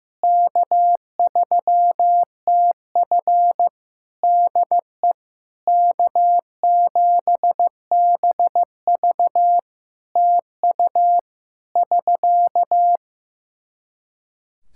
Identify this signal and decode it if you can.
Morse code, heard as a keyed tone, about 15 wpm, 700 Hz. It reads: K3TF DE K7BV TU <SK>
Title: Practice QSO - Exchange 4 - Ending - 15wpm